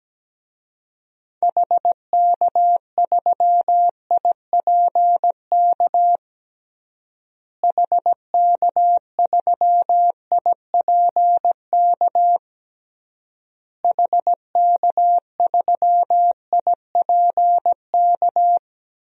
Morse code, heard as a keyed tone, about 17 wpm, 700 Hz. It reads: HK3IPK HK3IPK HK3IPK
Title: DXCC-100 Call Signs - 6 Characters - Repeated 3x - 17wpm